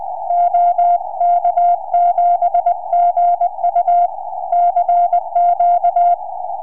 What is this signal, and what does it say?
Morse code, heard as a keyed tone, about 20 wpm, 690 Hz. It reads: OK7GU CQ
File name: ok7gu 4m